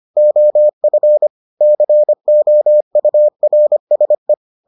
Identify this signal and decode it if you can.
Morse code, heard as a keyed tone, about 25 wpm, 600 Hz. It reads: OF COURSE